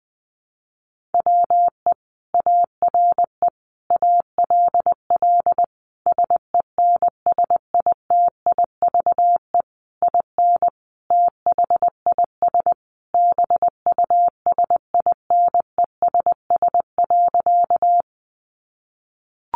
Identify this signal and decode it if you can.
Morse code, heard as a keyed tone, about 20 wpm, 700 Hz. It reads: WE ARE ALL SENSITIVE IN THIS BUSINESS.
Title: Sentences from Top 1400 Words 20wpm